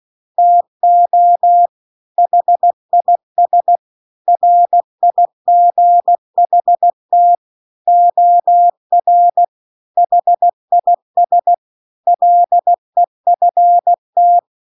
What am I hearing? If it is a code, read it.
Morse code, heard as a keyed tone, about 16 wpm, 700 Hz. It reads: TO HIS RIGHT OR HIS LEFT